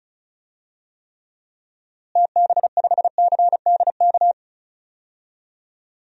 Morse code, heard as a keyed tone, about 35 wpm, 700 Hz. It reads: TB5CDK
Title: DXCC-100 Call Signs - 6 Characters - Repeated 2x - 35wpm